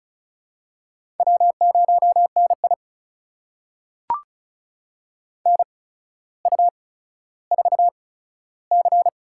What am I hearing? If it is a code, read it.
Morse code, heard as a keyed tone, about 36 wpm, spaced as for 15 wpm, 700 Hz. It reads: W0NI NU4C
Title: USA Call Signs - 4 Characters - 8x Character Spacing - 35wpm